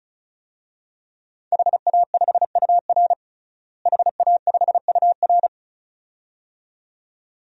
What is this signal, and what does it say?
Morse code, heard as a keyed tone, about 34 wpm, 700 Hz. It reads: HA5UR HA5UR
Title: DXCC-100 Call Signs - 5 Characters - Repeated 2x - 35wpm